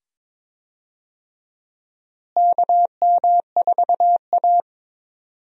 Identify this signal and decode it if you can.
Morse code, heard as a keyed tone, about 21 wpm, 700 Hz. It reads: KM4A
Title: USA Call Signs - 4 Characters - Repeated 2x - 22wpm